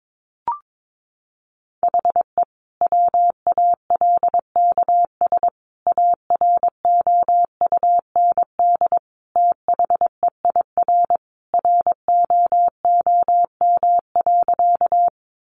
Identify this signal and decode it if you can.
Morse code, heard as a keyed tone, about 22 wpm, 700 Hz. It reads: HE WALKS AROUND THEIR ROOM.
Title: Easy Phrases 22wpm